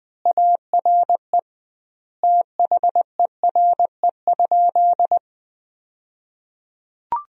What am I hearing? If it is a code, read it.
Morse code, heard as a keyed tone, about 20 wpm, 700 Hz. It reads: ARE THERE?
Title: 2x Word-Spacing - Sentences from Top 100 Words - 20wpm